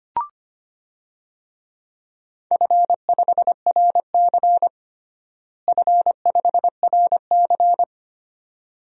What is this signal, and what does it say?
Morse code, heard as a keyed tone, about 25 wpm, 700 Hz. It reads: F5RC F5RC
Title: DXCC-100 Call Signs - 4 Characters - Repeated 2x - 25wpm